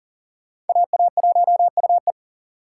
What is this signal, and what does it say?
Morse code, heard as a keyed tone, about 53 wpm, spaced as for 47 wpm, 700 Hz. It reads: AA1UE
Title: Speed Racing - USA 2x2 Call Signs 40wpm